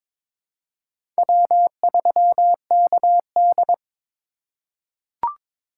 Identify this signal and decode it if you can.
Morse code, heard as a keyed tone, about 22 wpm, 700 Hz. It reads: W3KD
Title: USA Call Signs - 8x Character Spacing - 22wpm